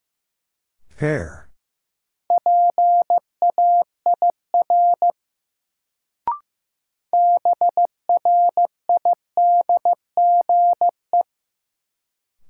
Morse code, heard as a keyed tone, about 15 wpm, 700 Hz. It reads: PAIR BRIDGE
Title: Sentences from Top 1200 Words - New Word Drill Alt - 15wpm